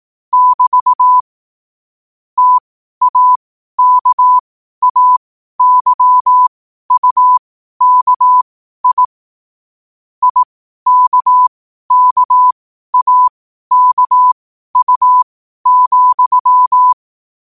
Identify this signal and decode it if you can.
Morse code, heard as a keyed tone, about 18 wpm, spaced as for 12 wpm, 1000 Hz. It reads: <BT> TAKAYUKI IKKAKU,
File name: qotdfiles / 12wpm